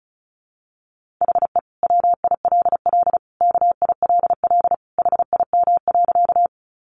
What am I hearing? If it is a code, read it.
Morse code, heard as a keyed tone, about 34 wpm, 700 Hz. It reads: HE WILL KILL HIM.
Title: Sentences from Top 1400 Words 35wpm